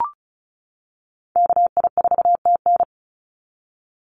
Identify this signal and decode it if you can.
Morse code, heard as a keyed tone, about 34 wpm, 700 Hz. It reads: KI4TN
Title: NA SOTA Chaser Callsigns 35wpm